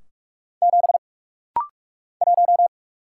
Morse code, heard as a keyed tone, about 44 wpm, 700 Hz. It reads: 7 1
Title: Single Letter-Number - Rapid-Fire 45wpm